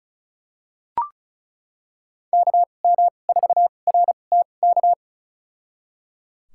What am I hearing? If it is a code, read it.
Morse code, heard as a keyed tone, about 36 wpm, spaced as for 25 wpm, 690 Hz. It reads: KM4RTK
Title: USA Call Signs - 6 Characters - 2x Character Spacing - 35wpm